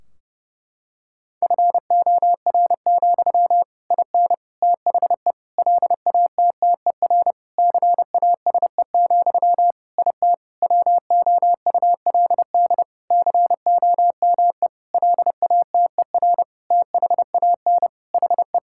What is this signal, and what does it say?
Morse code, heard as a keyed tone, about 30 wpm, 700 Hz. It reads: FOR, IN THE LATTER CASE, IT WOULD COME LATER THAN HE
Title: Sentences from Top 1600 Words 30wpm